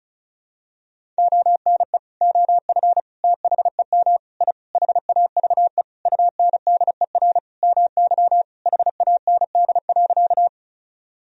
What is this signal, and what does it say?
Morse code, heard as a keyed tone, about 36 wpm, 700 Hz. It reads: ONE OF THEM I HAVE UNDER MY HAND.
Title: Sentences from Top 1000 Words - Review Part 1 - 35wpm